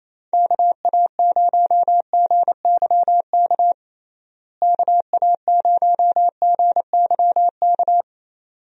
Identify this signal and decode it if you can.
Morse code, heard as a keyed tone, about 28 wpm, 700 Hz. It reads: KA0GYK KA0GYK
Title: USA Call Signs - 6 Characters - Repeated 2x - 28wpm